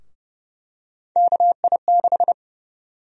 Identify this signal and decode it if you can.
Morse code, heard as a keyed tone, about 30 wpm, 700 Hz. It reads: KI6
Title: USA Call Sign Prefixes - Repeated 2x - 30wpm